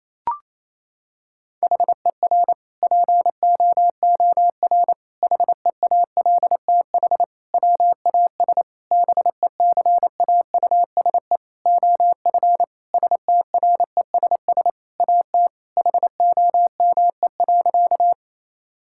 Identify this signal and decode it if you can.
Morse code, heard as a keyed tone, about 28 wpm, 700 Hz. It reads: HER POOR HEALTH WAS BECAUSE OF STRESS AT HOME.
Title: Sentences from Top 1000 Words - Review Part 6 - 28wpm